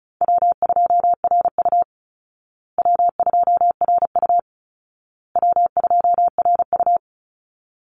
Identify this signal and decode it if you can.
Morse code, heard as a keyed tone, about 34 wpm, 700 Hz. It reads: W2RU W2RU W2RU
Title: Easy Call Signs Repeated 3x - 35wpm